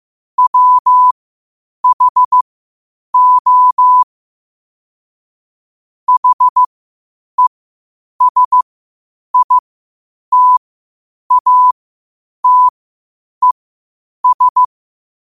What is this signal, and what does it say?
Morse code, heard as a keyed tone, about 15 wpm, spaced as for 8 wpm, 1000 Hz. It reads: WHO HESITATES
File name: qotdfiles / 8wpm